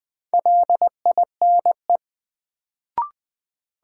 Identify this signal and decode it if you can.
Morse code, heard as a keyed tone, about 20 wpm, 700 Hz. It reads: LINE
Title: Sentences from Top 1200 Words - New Word Drill Alt - 20wpm